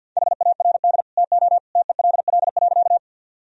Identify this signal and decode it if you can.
Morse code, heard as a keyed tone, about 51 wpm, 700 Hz. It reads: HARD TO TELL.
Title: Sentences from Top 1000 Words - Review Part 1 - 50wpm